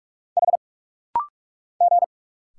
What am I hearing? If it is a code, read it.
Morse code, heard as a keyed tone, about 46 wpm, 700 Hz. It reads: H G
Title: Single Letters - Rapid-Fire 45wpm